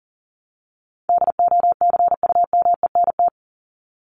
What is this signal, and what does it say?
Morse code, heard as a keyed tone, about 37 wpm, 700 Hz. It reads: DOCUMENT
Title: Top 1751-2000 Words 40wpm